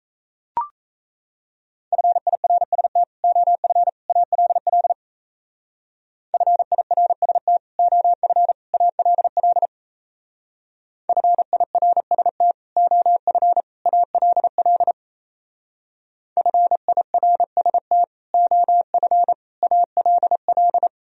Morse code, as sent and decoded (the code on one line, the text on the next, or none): ..-. .. .-. ... - / --- ..-. / .- .-.. .-.. / ..-. .. .-. ... - / --- ..-. / .- .-.. .-.. / ..-. .. .-. ... - / --- ..-. / .- .-.. .-.. / ..-. .. .-. ... - / --- ..-. / .- .-.. .-..
FIRST OF ALL FIRST OF ALL FIRST OF ALL FIRST OF ALL